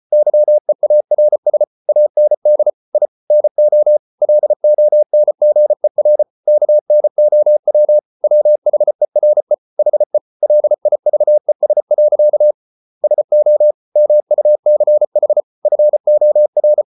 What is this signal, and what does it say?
Morse code, heard as a keyed tone, about 34 wpm, 600 Hz. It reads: YEARS AND I NO LONGER KNOW WHERE HE LIVES. SO MUCH FOR